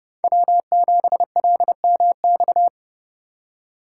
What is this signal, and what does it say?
Morse code, heard as a keyed tone, about 29 wpm, 700 Hz. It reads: W7LMX